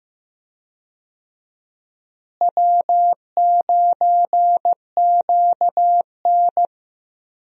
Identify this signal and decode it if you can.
Morse code, heard as a keyed tone, about 15 wpm, 700 Hz. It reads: W9QN
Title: USA Call Signs - 4 Characters - Repeated 2x - 15wpm